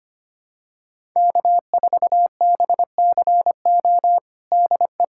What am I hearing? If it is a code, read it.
Morse code, heard as a keyed tone, about 25 wpm, 700 Hz. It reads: K4BCO DE
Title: USA Callsign DE Callsign - 5 Characters - 25wpm